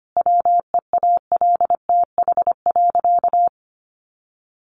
Morse code, heard as a keyed tone, about 25 wpm, 700 Hz. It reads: WEALTH.
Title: Sentences from Top 1200 Words 25wpm